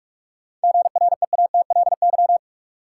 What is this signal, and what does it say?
Morse code, heard as a keyed tone, about 46 wpm, 700 Hz. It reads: GREATLY
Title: Sentences from Top 1400 Words - New Word Drill - 45wpm